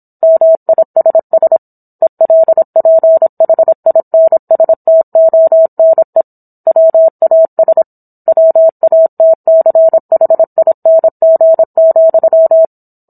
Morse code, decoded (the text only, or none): MISS ELPHINSTONE WAS WATCHING,